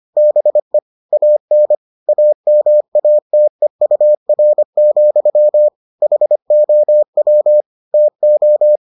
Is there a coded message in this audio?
Yes: Morse code from 0.2 to 8.8 s, heard as a keyed tone, about 25 wpm, 600 Hz. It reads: BE AN AMATEUR, HOW TO